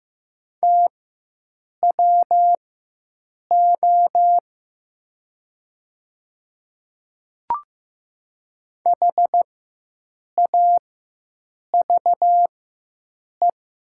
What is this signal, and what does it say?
Morse code, heard as a keyed tone, about 15 wpm, spaced as for 7 wpm, 700 Hz. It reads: TWO HAVE
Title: 4x Character Spacing - Top 100 Words - 15wpm